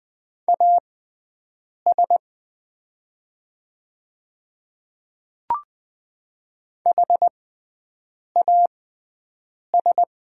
Morse code, heard as a keyed tone, about 20 wpm, spaced as for 7 wpm, 700 Hz. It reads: AS HAS
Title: 6x Character Spacing - Top 200 Words - 20wpm